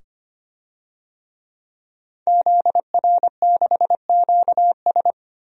Morse code, heard as a keyed tone, about 25 wpm, 700 Hz. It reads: ZR6QS
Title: DXCC-100 Call Signs - 5 Characters - 25wpm